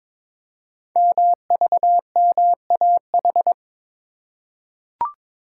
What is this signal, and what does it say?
Morse code, heard as a keyed tone, about 22 wpm, 700 Hz. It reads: MVMAH
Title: Code Groups - 5 Letters - 22wpm